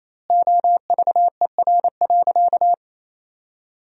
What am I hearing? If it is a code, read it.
Morse code, heard as a keyed tone, about 28 wpm, 700 Hz. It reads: OVER.
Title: 2x Word-Spacing - Sentences from Top 300 Words - 28wpm